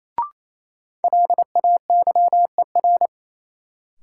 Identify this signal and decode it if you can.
Morse code, heard as a keyed tone, about 28 wpm, 700 Hz. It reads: LAYER
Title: Sentences from Top 1600 Words - New Word Drill - 28wpm